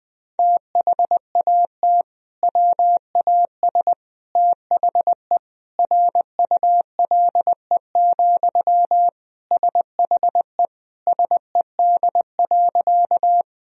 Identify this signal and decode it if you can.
Morse code, heard as a keyed tone, about 21 wpm, 700 Hz. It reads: THAT WAS THE RULE, SHE SED.
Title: Sentences from Top 400 Words and Abbreviations 20wpm